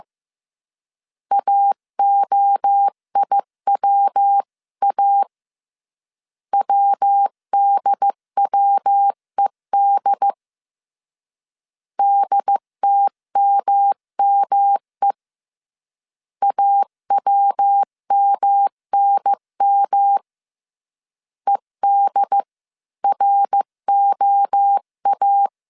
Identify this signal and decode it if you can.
Morse code, heard as a keyed tone, about 15 wpm, 800 Hz. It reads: AOIWA WDWED DTMME AWMNM EDROA
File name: CW / DET / 1 3 1500C S C 14wpm 2458